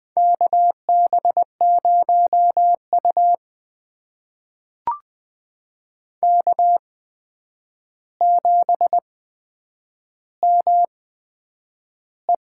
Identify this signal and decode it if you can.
Morse code, heard as a keyed tone, about 20 wpm, spaced as for 9 wpm, 700 Hz. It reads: KB0U K7ME
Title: USA Call Signs - 4 Characters - 8x Character Spacing - 20wpm